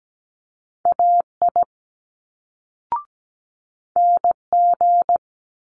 Morse code, heard as a keyed tone, about 17 wpm, 700 Hz. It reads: AI NG
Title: Multi-letter Phonemes Quick Fire 17wpm